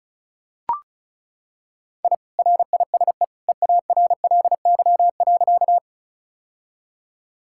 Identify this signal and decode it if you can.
Morse code, heard as a keyed tone, about 36 wpm, 700 Hz. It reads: I RISE EARLY.
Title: Sentences from Top 1000 Words - Review Part 6 - 35wpm